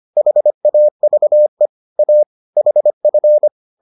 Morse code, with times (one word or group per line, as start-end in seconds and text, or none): 0.2-1.7 HAVE
2.0-2.2 A
2.6-3.5 HF